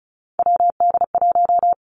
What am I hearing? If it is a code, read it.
Morse code, heard as a keyed tone, about 36 wpm, 700 Hz. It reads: WD1
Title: Three Challenging Letter-Number Combinations 35wpm